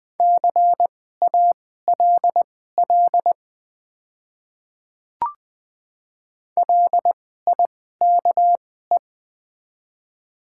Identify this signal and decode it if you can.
Morse code, heard as a keyed tone, about 20 wpm, spaced as for 15 wpm, 690 Hz. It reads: CALL LIKE
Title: 2x Character Spacing - Top 100 Words - 20wpm